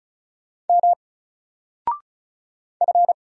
Morse code, heard as a keyed tone, about 36 wpm, 700 Hz. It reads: M F
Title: Single Letter-Number Alt 35wpm